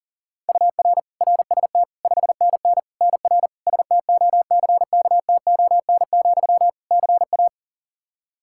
Morse code, heard as a keyed tone, about 40 wpm, 700 Hz. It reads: UR RST 5NN NR STOCKTON, CA
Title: Practice QSO - Exchange 1 - Beginning - 40wpm